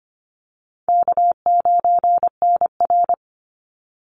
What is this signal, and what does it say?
Morse code, heard as a keyed tone, about 25 wpm, 700 Hz. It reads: K9NR
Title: Easy Call Signs - 25wpm